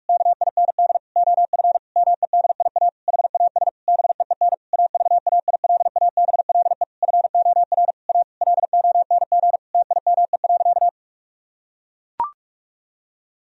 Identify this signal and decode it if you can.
Morse code, heard as a keyed tone, about 41 wpm, 700 Hz. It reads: KIND OF MEDIA HAS BEEN AVAILABLE FOR A LONG TIME.
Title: Sentences from Top 1000 Words - Review Part 4 - 45wpm